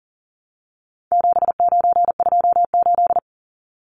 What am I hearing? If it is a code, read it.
Morse code, heard as a keyed tone, about 40 wpm, 700 Hz. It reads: 7928
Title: Numbers - 4-Digit 40wpm